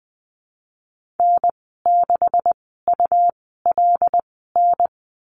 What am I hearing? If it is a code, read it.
Morse code, heard as a keyed tone, about 20 wpm, spaced as for 14 wpm, 700 Hz. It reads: N6ULN
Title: 128 Lesson 21g - Number 6 - Call Signs 20wpm